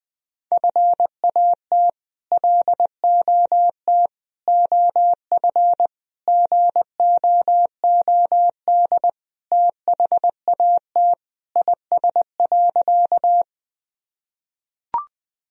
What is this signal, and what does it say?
Morse code, heard as a keyed tone, about 21 wpm, 700 Hz. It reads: FAT LOT OF GOOD THAT IS.
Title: Sentences from Top 1000 Words - Review Part 4 - 20wpm